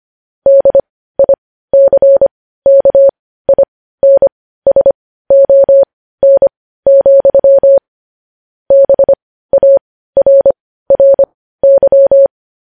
Morse code, heard as a keyed tone, about 25 wpm, spaced as for 15 wpm, 550 Hz. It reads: DICKINSON, BARRY
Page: news-headlines-in-morse-code-at-15-wpm